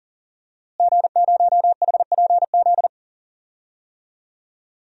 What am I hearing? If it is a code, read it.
Morse code, heard as a keyed tone, about 40 wpm, 700 Hz. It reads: G0HPZ
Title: DXCC-100 Call Signs - 5 Characters - Repeated 2x - 40wpm